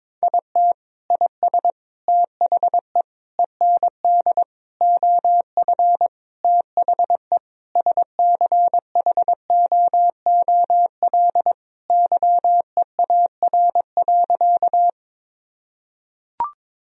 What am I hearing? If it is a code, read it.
Morse code, heard as a keyed tone, about 22 wpm, 700 Hz. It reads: IT IS THE END OF THE SCHOOL YEAR.
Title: Sentences from Top 1000 Words - Review Part 1 - 22wpm